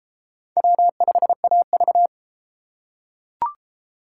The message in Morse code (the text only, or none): W5AV